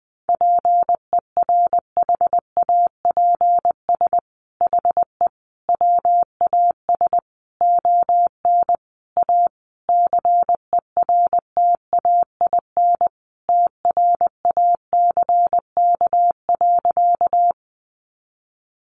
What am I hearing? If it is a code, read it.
Morse code, heard as a keyed tone, about 20 wpm, 700 Hz. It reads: PERHAPS HE WAS ON A CERTAIN TRACK.
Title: Sentences from Top 1100 Words 20wpm